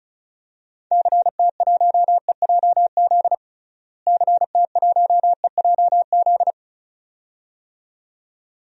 Morse code, heard as a keyed tone, about 36 wpm, 700 Hz. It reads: CT1EJZ CT1EJZ
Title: DXCC-100 Call Signs - 6 Characters - Repeated 2x - 35wpm